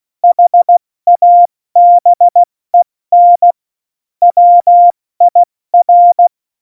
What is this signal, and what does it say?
Morse code, heard as a keyed tone, about 16 wpm, spaced as for 14 wpm, 700 Hz. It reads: HABEN WIR
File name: Morsekurs / Witze / Witz 06. November 2018, 70 F80